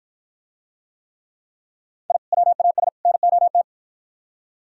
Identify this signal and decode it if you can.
Morse code, heard as a keyed tone, about 48 wpm, 700 Hz. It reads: I WAS NOT